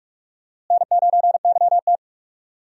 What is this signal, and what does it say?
Morse code, heard as a keyed tone, about 46 wpm, 700 Hz. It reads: N9YT
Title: USA Call Signs - 4 Characters - Repeated 2x - 45wpm